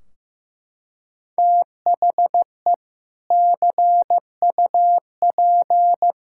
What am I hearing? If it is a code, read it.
Morse code, heard as a keyed tone, about 15 wpm, 700 Hz. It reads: THE CUP
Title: Sentences from Top 1600 Words 15wpm